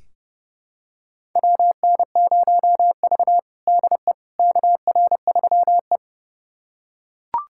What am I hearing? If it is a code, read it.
Morse code, heard as a keyed tone, about 30 wpm, 700 Hz. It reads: WN0V DE KR3E
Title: USA Callsign DE Callsign - 4 Characters - 30wpm